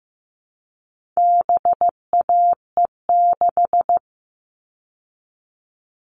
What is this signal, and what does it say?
Morse code, heard as a keyed tone, about 15 wpm, 700 Hz. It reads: BAE6
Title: All Cut Numbers - 4-Digit 15wpm